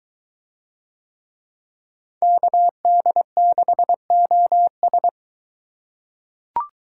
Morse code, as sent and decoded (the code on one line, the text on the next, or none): -.- -.. -.... --- ...
KD6OS